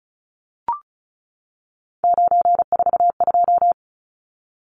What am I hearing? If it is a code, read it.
Morse code, heard as a keyed tone, about 34 wpm, 700 Hz. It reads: 942
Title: Numbers - 3-Digit 35wpm